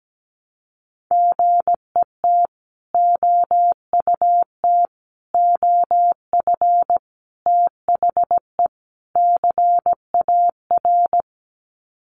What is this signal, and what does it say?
Morse code, heard as a keyed tone, about 17 wpm, 700 Hz. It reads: GET OUT OF THE CAR
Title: Sets of 5 Words Encore2 17wpm